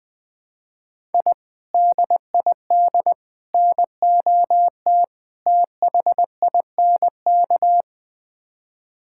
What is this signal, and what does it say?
Morse code, heard as a keyed tone, about 20 wpm, 700 Hz. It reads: I DID NOT THINK
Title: Build A Phrase - 4-Word Phrases - 20wpm